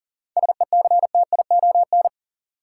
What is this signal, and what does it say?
Morse code, heard as a keyed tone, about 40 wpm, 700 Hz. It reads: SECTION
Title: Sentences from Top 1000 Words - Review Part 5 - New Word Drill - 40wpm